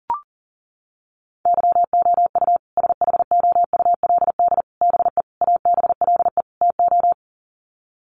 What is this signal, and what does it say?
Morse code, heard as a keyed tone, about 40 wpm, 700 Hz. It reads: YOU SHOULD BE ABLE TO
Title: Sets of 5 Words Encore 40wpm